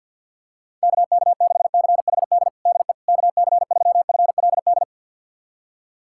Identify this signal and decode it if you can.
Morse code, heard as a keyed tone, about 48 wpm, 700 Hz. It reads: KK6CHD DE KC3FLD
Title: USA Callsign DE Callsign - 6 Characters - 50wpm